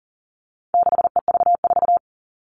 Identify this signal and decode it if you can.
Morse code, heard as a keyed tone, about 40 wpm, 700 Hz. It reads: 6EV4
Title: All Cut Numbers - 4-Digit 40wpm